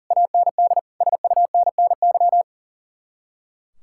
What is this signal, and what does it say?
Morse code, heard as a keyed tone, about 36 wpm, 700 Hz. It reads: AND SUNNY
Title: Speed-Racing Common QSO Phrases Encore - 40wpm